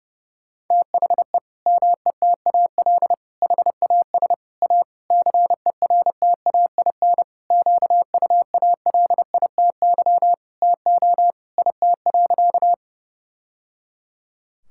Morse code, heard as a keyed tone, about 32 wpm, 700 Hz. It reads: THE METAL HAS A CERTAIN QUALITY TO IT.
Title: Sentences from Top 1000 Words - Review Part 6 - 30wpm